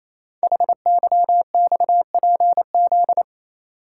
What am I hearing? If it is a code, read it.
Morse code, heard as a keyed tone, about 28 wpm, 700 Hz. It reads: HYXPZ
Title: Code Groups - 5 Letters - 28wpm